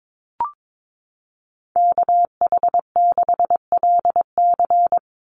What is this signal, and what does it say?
Morse code, heard as a keyed tone, about 22 wpm, 700 Hz. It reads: KH6LC
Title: Call Signs Repeated 3x - 22wpm